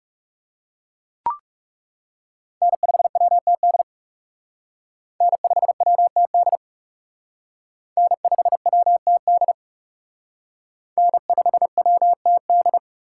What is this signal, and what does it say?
Morse code, heard as a keyed tone, about 37 wpm, 700 Hz. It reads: N5WTD N5WTD N5WTD N5WTD
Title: Speed Racing - USA 1x3 Call Signs 30wpm